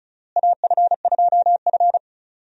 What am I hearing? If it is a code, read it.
Morse code, heard as a keyed tone, about 34 wpm, 700 Hz. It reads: AF2F
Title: USA Call Signs - 4 Characters - Repeated 3x - 35wpm